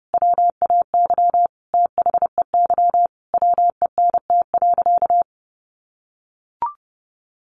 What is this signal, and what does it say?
Morse code, heard as a keyed tone, about 30 wpm, 700 Hz. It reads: WAY THEY WENT.
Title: Sentences from Top 1300 Words 30wpm